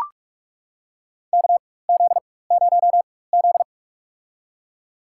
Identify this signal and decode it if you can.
Morse code, heard as a keyed tone, about 42 wpm, spaced as for 21 wpm, 700 Hz. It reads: KZ0Z
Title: USA Call Signs - 4 Characters - 4x Character Spacing - 45wpm